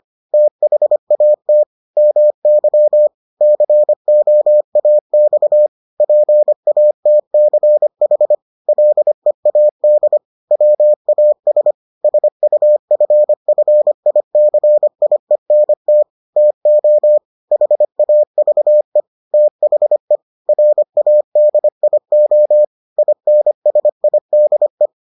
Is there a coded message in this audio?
Yes: Morse code from 0.3 to 24.9 s, heard as a keyed tone, about 25 wpm, 600 Hz. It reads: THAT MY COAX PATCH LEAD WAS SUFFICIENT TO HAVE THE RADIO INSIDE